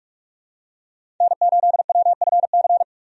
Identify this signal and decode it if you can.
Morse code, heard as a keyed tone, about 42 wpm, 700 Hz. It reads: N8WFC
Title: USA Call Signs - 5 Characters - 45wpm